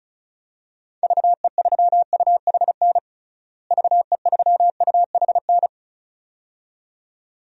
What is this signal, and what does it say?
Morse code, heard as a keyed tone, about 36 wpm, 700 Hz. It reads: VE3UHN VE3UHN
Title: DXCC-100 Call Signs - 6 Characters - Repeated 2x - 35wpm